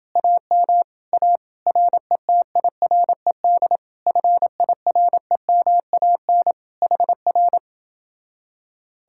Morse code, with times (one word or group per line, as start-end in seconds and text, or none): 0.2-0.8 AM
1.1-1.4 A
1.7-3.8 RETIRED
4.1-6.5 FIREMAN
6.8-7.6 HR